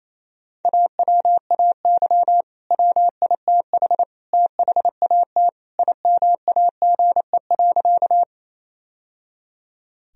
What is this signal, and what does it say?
Morse code, heard as a keyed tone, about 28 wpm, 700 Hz. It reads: AWAY WITH THAT IMAGE.
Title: Sentences from Top 1000 Words - Review Part 4 - 28wpm